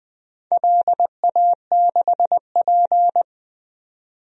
Morse code, heard as a keyed tone, about 20 wpm, 700 Hz. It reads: LA6P
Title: DXCC-100 Call Signs - 4 Characters - 20wpm